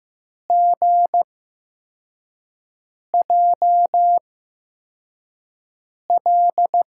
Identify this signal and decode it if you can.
Morse code, heard as a keyed tone, about 15 wpm, spaced as for 4 wpm, 700 Hz. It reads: GJL
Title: USA Call Signs - 5 Characters - 8x Character Spacing - 15wpm